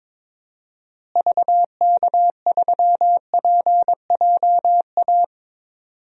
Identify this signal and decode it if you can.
Morse code, heard as a keyed tone, about 22 wpm, 700 Hz. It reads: VK3PJA
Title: DXCC-100 Call Signs - 6 Characters - Repeated 2x - 22wpm